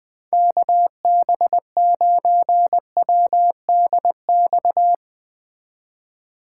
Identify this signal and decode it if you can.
Morse code, heard as a keyed tone, about 19 wpm, 700 Hz. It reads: KB9WDX
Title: Speed Racing - USA Call Signs 17wpm